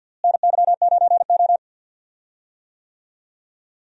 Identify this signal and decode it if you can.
Morse code, heard as a keyed tone, about 49 wpm, 700 Hz. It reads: NY9O
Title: USA Call Signs - 4 Characters - Repeated 3x - 50wpm